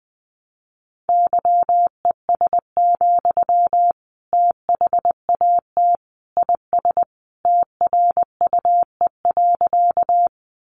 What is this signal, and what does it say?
Morse code, heard as a keyed tone, about 20 wpm, 700 Hz. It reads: YES, THAT IS TRUE.
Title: Sentences from Top 500 Words 20wpm